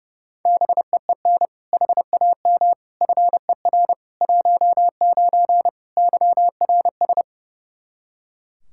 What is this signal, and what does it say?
Morse code, heard as a keyed tone, about 30 wpm, 700 Hz. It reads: BEEN HAM FER 19 YRS
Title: Practice QSO - Exchange 3 - More Info - 30wpm